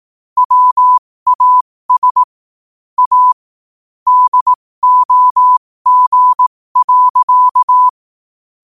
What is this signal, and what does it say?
Morse code, heard as a keyed tone, about 18 wpm, spaced as for 15 wpm, 1000 Hz. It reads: WAS A DOG.
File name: qotdfiles / 15wpm